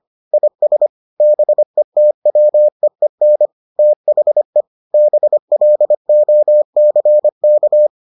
Morse code, heard as a keyed tone, about 25 wpm, 600 Hz. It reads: IS BETWEEN THE BLOCK